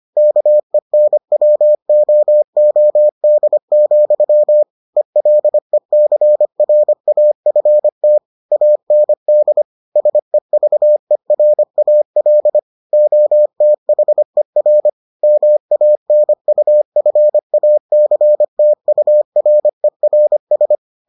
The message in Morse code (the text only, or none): KENWOOD, ELECRAFT AND SEVERAL OTHER MANUFACTURERS